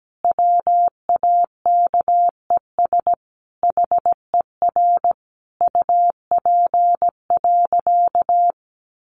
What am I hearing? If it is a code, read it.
Morse code, heard as a keyed tone, about 17 wpm, 700 Hz. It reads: WAKES HER UP.
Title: Easy Phrases 17wpm